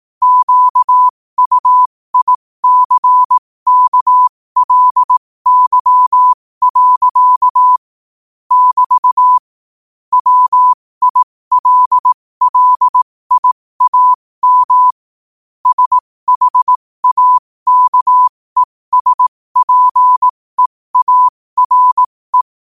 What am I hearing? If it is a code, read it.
Morse code, heard as a keyed tone, about 18 wpm, spaced as for 15 wpm, 1000 Hz. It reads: QUICKLY. <BT> WILLIAM SHAKESPEARE
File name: qotdfiles / 15wpm